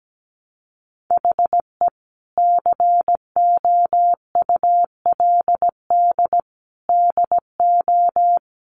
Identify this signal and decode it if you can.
Morse code, heard as a keyed tone, about 17 wpm, 700 Hz. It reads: HE COULD DO
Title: Sentences from Top 1500 Words 17wpm